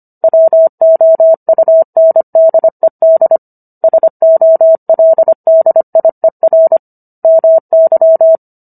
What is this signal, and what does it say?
Morse code, heard as a keyed tone, about 25 wpm, 650 Hz. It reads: WOUNDED SOLDIER MY